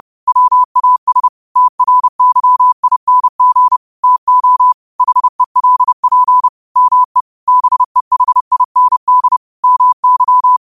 Morse code, heard as a keyed tone, about 30 wpm, 1000 Hz. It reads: WAS TRYING TO HELP ME BEHIND MY